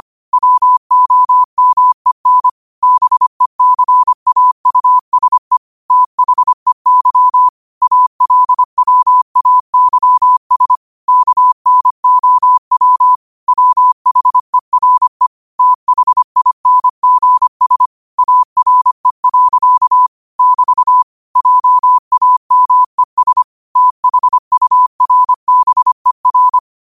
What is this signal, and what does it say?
Morse code, heard as a keyed tone, about 25 wpm, 1000 Hz. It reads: WOMEN BECAUSE THEY ALWAYS KNOW WHERE THINGS ARE. <BT> JAMES THURBER